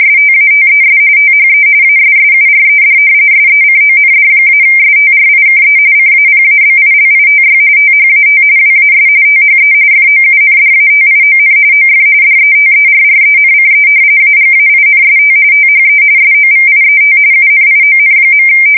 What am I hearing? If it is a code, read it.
RTTY, about 45.45 baud. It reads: STATES THE ASSURED SECOND-STRIKE CAPABILITY THAT WAS THE FOUNDATION OF DETERRENCE AGAINST THE SOVIET UNION.
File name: fsk170 / usnavy5